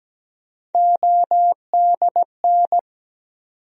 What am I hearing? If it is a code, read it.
Morse code, heard as a keyed tone, about 17 wpm, 700 Hz. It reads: ODN
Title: USA Call Sign Postfixes - 3 Letters 17wpm